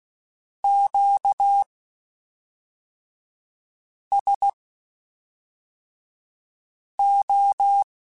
Morse code, heard as a keyed tone, about 16 wpm, spaced as for 3 wpm, 800 Hz. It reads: QSO
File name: (spaced) / Q-Calls Spaced-mp3s / Q Calls Twice sp @5